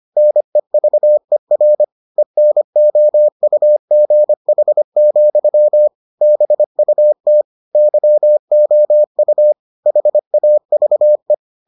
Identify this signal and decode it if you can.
Morse code, heard as a keyed tone, about 25 wpm, 600 Hz. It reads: NEVER ENOUGH, BUT YOU HAVE